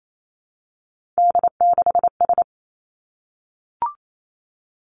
Morse code, heard as a keyed tone, about 28 wpm, 700 Hz. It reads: D6S
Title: Three Challenging Letter-Number Combinations 28wpm